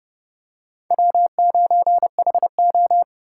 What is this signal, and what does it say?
Morse code, heard as a keyed tone, about 30 wpm, 700 Hz. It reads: W9HO
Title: USA Call Signs - 4 Characters - 4x Character Spacing - 30wpm